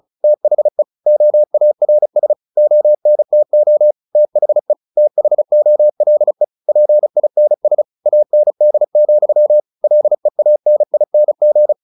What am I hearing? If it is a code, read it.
Morse code, heard as a keyed tone, about 34 wpm, 600 Hz. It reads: THE OARS ONTO THE THOLE PINS AND, LEANING
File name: the old man and the sea / the old man and the sea 1 35wpm